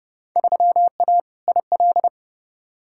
Morse code, heard as a keyed tone, about 32 wpm, 700 Hz. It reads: 3A IL
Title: Field Day Exchanges - 30wpm